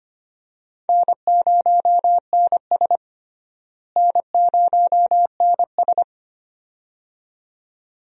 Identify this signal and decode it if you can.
Morse code, heard as a keyed tone, about 25 wpm, 700 Hz. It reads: N0NS N0NS
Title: USA Call Signs - 4 Characters - Repeated 2x - 25wpm